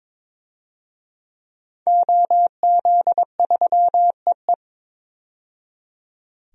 Morse code, heard as a keyed tone, about 22 wpm, 700 Hz. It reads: OZ3EE
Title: DXCC-100 Call Signs - 5 Characters - 22wpm